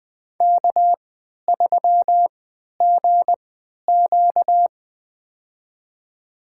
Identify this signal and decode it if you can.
Morse code, heard as a keyed tone, about 20 wpm, spaced as for 11 wpm, 700 Hz. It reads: K3GQ
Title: USA Call Signs - 4 Characters - 3x Character Spacing - 20wpm